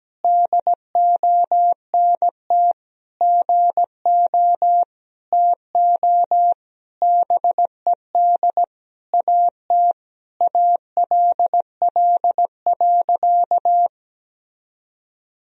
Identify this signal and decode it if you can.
Morse code, heard as a keyed tone, about 17 wpm, 700 Hz. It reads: DONT GO TO BED AT ALL.
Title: Sentences from Top 1000 Words - Review Part 3 - 17wpm